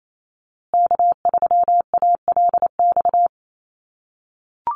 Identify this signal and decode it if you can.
Morse code, heard as a keyed tone, about 28 wpm, 700 Hz. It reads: K3ALX
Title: USA Call Signs Repeated 3x 28wpm